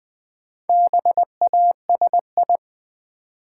Morse code, heard as a keyed tone, about 20 wpm, 700 Hz. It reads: BASI
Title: Build A Word - 1 Letter At A Time - 5-Letter Words - 20wpm